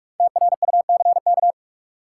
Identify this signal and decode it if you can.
Morse code, heard as a keyed tone, about 41 wpm, 700 Hz. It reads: TRUCK
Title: Five Letter Words - 45wpm